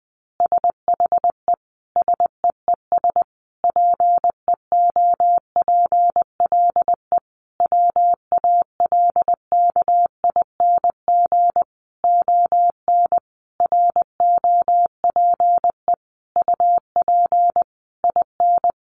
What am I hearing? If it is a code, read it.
Morse code, heard as a keyed tone, about 20 wpm, 700 Hz. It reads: SHE SEES PEOPLE WALKING ON ROPE UP IN
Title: Easy Phrases 20wpm